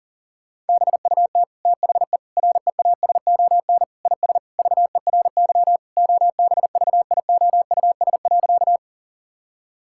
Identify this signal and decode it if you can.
Morse code, heard as a keyed tone, about 37 wpm, 700 Hz. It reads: BUT THE REASON IS VERY OBVIOUS.
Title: Sentences from Top 1600 Words 40wpm